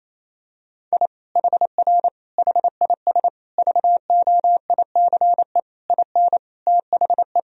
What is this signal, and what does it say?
Morse code, heard as a keyed tone, about 28 wpm, 700 Hz. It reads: I HR HIS VOICE IN THE
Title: Sentences from Top 500 Words and Abbreviations 28wpm